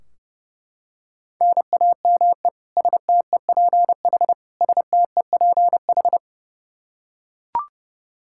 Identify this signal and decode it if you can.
Morse code, heard as a keyed tone, about 29 wpm, 700 Hz. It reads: NAME STEPH STEPH
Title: Practice QSO - Exchange 1 - Beginning - 30wpm